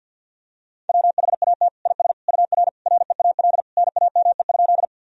Morse code, heard as a keyed tone, about 47 wpm, 700 Hz. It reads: WHAT IS UR REAL NAME?
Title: Sentences from Top 300 Words and Abbreviations 50wpm